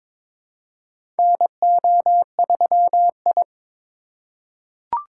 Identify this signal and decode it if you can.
Morse code, heard as a keyed tone, about 22 wpm, 700 Hz. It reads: NO3I
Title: USA Call Signs - 4 Characters - 4x Character Spacing - 22wpm